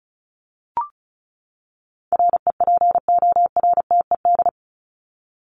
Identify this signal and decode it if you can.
Morse code, heard as a keyed tone, about 34 wpm, 700 Hz. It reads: REPORTED
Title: ED Suffix - Top 100 Words with Base Words 35wpm